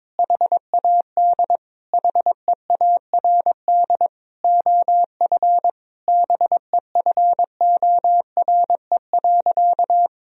Morse code, heard as a keyed tone, about 22 wpm, 700 Hz. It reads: HAD HEARD OF BEFORE.
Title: Sentences from Top 1000 Words - Review Part 3 - 22wpm